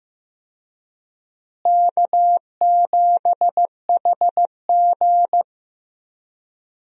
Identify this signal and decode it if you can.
Morse code, heard as a keyed tone, about 15 wpm, 700 Hz. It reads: K7HG
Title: USA Call Signs - 4 Characters - Repeated 2x - 15wpm